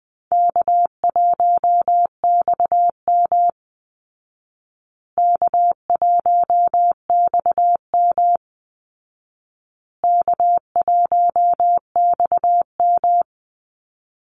Morse code, heard as a keyed tone, about 20 wpm, 700 Hz. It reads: K1XM K1XM K1XM
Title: Easy Call Signs Repeated 3x - 20wpm